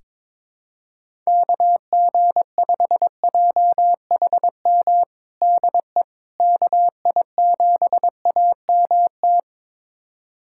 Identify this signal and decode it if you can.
Morse code, heard as a keyed tone, about 22 wpm, 700 Hz. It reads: KG5JHM DE KI7AMT
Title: USA Callsign DE Callsign - 6 Characters - 22wpm